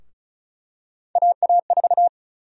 Morse code, heard as a keyed tone, about 36 wpm, 700 Hz. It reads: AA4
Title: USA Call Sign Prefixes - 35wpm